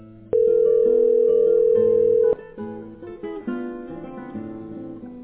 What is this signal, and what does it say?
Ringback tone (North America), 440 + 480 Hz.